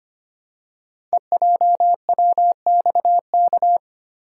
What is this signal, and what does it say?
Morse code, heard as a keyed tone, about 25 wpm, 700 Hz. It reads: EJWXK
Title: Code Groups - 5 Letters - 25wpm